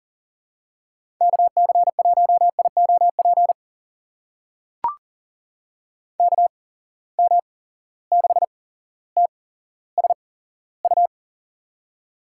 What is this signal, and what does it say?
Morse code, heard as a keyed tone, about 43 wpm, spaced as for 17 wpm, 700 Hz. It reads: KC1IOP KM6TSU
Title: USA Call Signs - 6 Characters - 8x Character Spacing - 40wpm